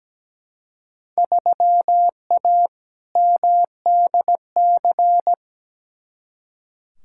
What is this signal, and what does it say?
Morse code, heard as a keyed tone, about 17 wpm, 700 Hz. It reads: 3A MDC
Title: Field Day Exchanges - 17wpm